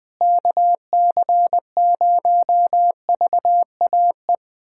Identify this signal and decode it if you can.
Morse code, heard as a keyed tone, about 20 wpm, 700 Hz. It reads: KC0VAE
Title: USA Callsign DE Callsign - 6 Characters - 20wpm